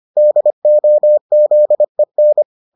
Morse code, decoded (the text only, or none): DOZEN